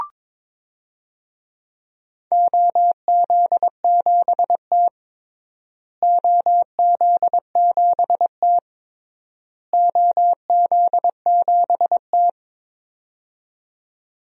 Morse code, heard as a keyed tone, about 22 wpm, 700 Hz. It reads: OZ7T OZ7T OZ7T
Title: DXCC-100 Call Signs - 4 Characters - Repeated 3x - 22wpm